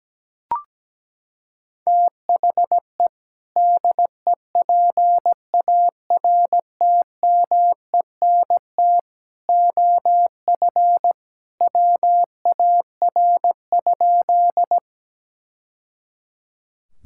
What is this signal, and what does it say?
Morse code, heard as a keyed tone, about 17 wpm, 700 Hz. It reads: THE DEPARTMENT OF WAR?
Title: Sentences from Top 1000 Words - Review Part 5 - 17wpm